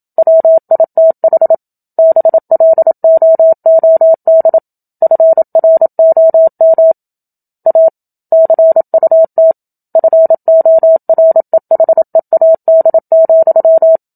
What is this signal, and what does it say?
Morse code, heard as a keyed tone, about 27 wpm, 650 Hz. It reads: WITH BLOOD FROM A CUT FOREHEAD,
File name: Book Two / 01-Chapter 1 - 27 WPM